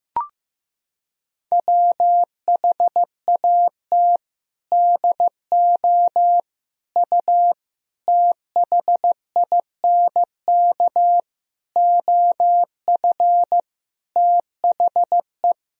Morse code, heard as a keyed tone, about 15 wpm, 700 Hz. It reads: WHAT DO U THINK OF THE
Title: Sentences from Top 300 Words and Abbreviations 15wpm